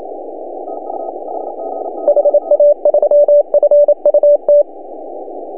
Morse code, heard as a keyed tone, about 31 wpm, 590 Hz. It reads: HA3FUT
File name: CW / HA2OS RHOB 2017 CW